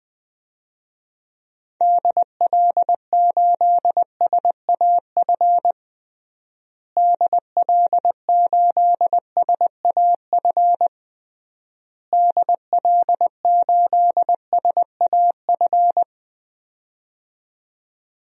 Morse code, heard as a keyed tone, about 20 wpm, 700 Hz. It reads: DL8SAF DL8SAF DL8SAF
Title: DXCC-100 Call Signs - 6 Characters - Repeated 3x - 20wpm